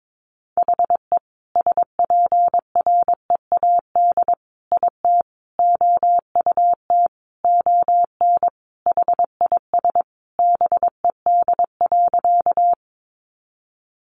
Morse code, heard as a keyed tone, about 22 wpm, 700 Hz. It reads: HE SPREAD IT OUT ON HIS BED.
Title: Sentences from Top 1200 Words 22wpm